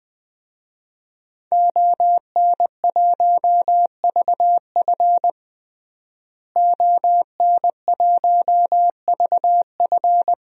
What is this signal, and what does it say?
Morse code, heard as a keyed tone, about 20 wpm, 700 Hz. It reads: ON1VF ON1VF
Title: DXCC-100 Call Signs - 5 Characters - Repeated 3x - 20wpm